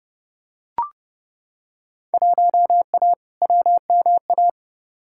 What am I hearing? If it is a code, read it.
Morse code, heard as a keyed tone, about 32 wpm, 700 Hz. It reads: 1A WMA
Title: Field Day Exchanges - 30wpm